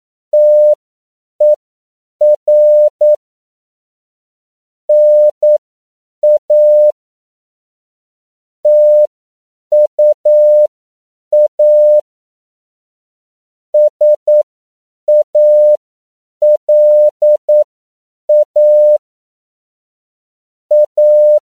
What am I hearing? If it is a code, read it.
Morse code, heard as a keyed tone, about 9 wpm, spaced as for 7 wpm, 600 Hz. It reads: TER NA TUA SALA A